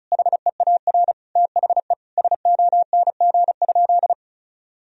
Morse code, heard as a keyed tone, about 36 wpm, 700 Hz. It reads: HEAR THE SONG?
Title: Sentences from Top 1000 Words - Review Part 2 - 35wpm